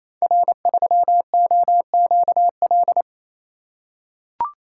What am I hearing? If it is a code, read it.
Morse code, heard as a keyed tone, about 27 wpm, 700 Hz. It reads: R3OQL
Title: Code Groups - 5 Letter:Numbers - 28wpm